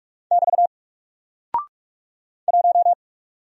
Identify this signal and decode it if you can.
Morse code, heard as a keyed tone, about 44 wpm, 700 Hz. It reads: <BT> 1
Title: Single Character (Letter:Number:Punctuation:Prosign) 45wpm